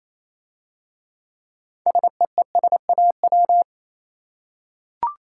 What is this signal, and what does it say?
Morse code, heard as a keyed tone, about 27 wpm, 700 Hz. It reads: SEESAW
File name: Easy Words - Speed-Racing with Hints - 22wpm